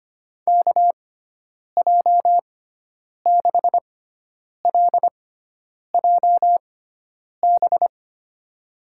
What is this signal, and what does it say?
Morse code, heard as a keyed tone, about 25 wpm, spaced as for 9 wpm, 700 Hz. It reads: KJ6LJB
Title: USA Call Signs - 6 Characters - 6x Character Spacing - 25wpm